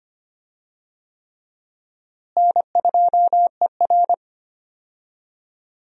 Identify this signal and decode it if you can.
Morse code, heard as a keyed tone, about 25 wpm, 700 Hz. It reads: N2ER
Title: USA Call Signs - 4 Characters - Repeated 3x - 25wpm